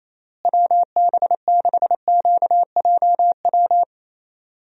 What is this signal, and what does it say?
Morse code, heard as a keyed tone, about 27 wpm, 700 Hz. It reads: WB6QJW
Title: USA Call Signs - 6 Characters - Repeated 3x - 28wpm